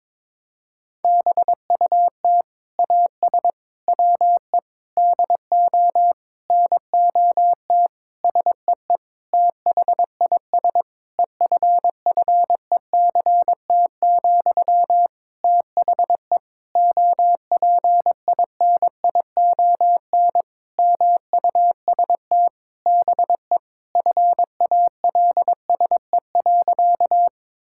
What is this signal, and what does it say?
Morse code, heard as a keyed tone, about 22 wpm, 700 Hz. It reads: BUT AS WE DO NOT SEE THIS EFFECT, THE OPINION MUST BE FALSE.
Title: Sentences from Top 1600 Words 22wpm